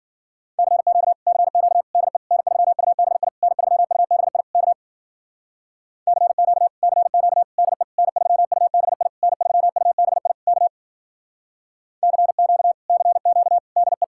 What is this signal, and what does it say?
Morse code, heard as a keyed tone, about 52 wpm, 700 Hz. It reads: CQ CQ DE N3UBI N3UBI K CQ CQ DE N3UBI N3UBI K CQ CQ DE